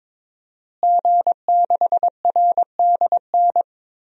Morse code, heard as a keyed tone, about 22 wpm, 700 Hz. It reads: G6RDN
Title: DXCC-100 Call Signs - 5 Characters - Repeated 2x - 22wpm